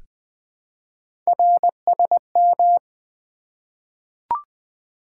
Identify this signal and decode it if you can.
Morse code, heard as a keyed tone, about 20 wpm, 700 Hz. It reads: RSM
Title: Three Letter-Number Combinations - 20wpm